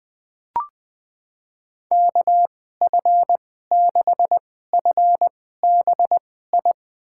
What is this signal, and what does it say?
Morse code, heard as a keyed tone, about 20 wpm, spaced as for 15 wpm, 700 Hz. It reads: KF6FBI
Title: USA Call Signs - 6 Characters - 2x Character Spacing - 20wpm